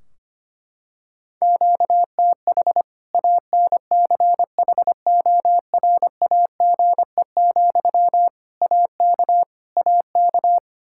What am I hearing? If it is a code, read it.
Morse code, heard as a keyed tone, about 25 wpm, 700 Hz. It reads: QTH ANCHORAGE, AK AK
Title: Practice QSO - Exchange 1 - Beginning - 25wpm